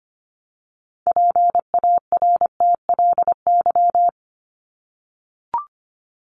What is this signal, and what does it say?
Morse code, heard as a keyed tone, about 25 wpm, 700 Hz. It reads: PARTLY
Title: LY Suffix - Top 100 Words with Base Words 25wpm